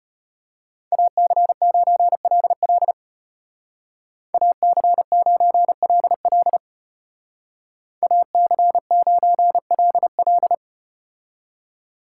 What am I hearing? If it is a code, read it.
Morse code, heard as a keyed tone, about 34 wpm, 700 Hz. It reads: AC9LL AC9LL AC9LL